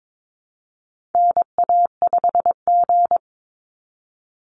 Morse code, heard as a keyed tone, about 22 wpm, 700 Hz. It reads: NA5G
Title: Call Signs Repeated 3x - 22wpm